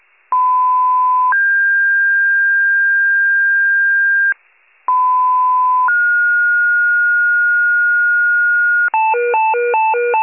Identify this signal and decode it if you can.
Two-tone page, heard as a one-second tone, then a three-second tone, about 1010 then 1600 Hz, about 1010 then 1430 Hz, with noise. It is heard from 0.3 to 8.9 s.
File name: pager / 94 RESCUE DUTY / wcalert